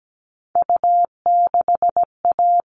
Morse code, heard as a keyed tone, about 17 wpm, 700 Hz. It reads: U6A
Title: All Cut Numbers - 3-Digit 17wpm